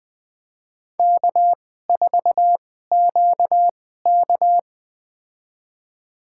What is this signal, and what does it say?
Morse code, heard as a keyed tone, about 20 wpm, spaced as for 14 wpm, 700 Hz. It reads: K4QK
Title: USA Call Signs - 4 Characters - 2x Character Spacing - 20wpm